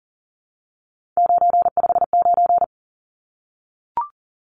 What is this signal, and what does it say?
Morse code, heard as a keyed tone, about 40 wpm, 700 Hz. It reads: 959
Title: Numbers - 3-Digit 40wpm